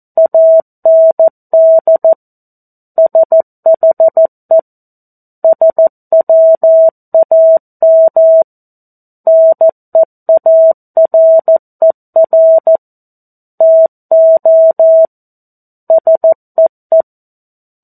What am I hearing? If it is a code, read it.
Morse code, heard as a keyed tone, about 14 wpm, 650 Hz. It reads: AND SHE SWAM NEARER TO SEE
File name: 02-Chapter-02 14 WPM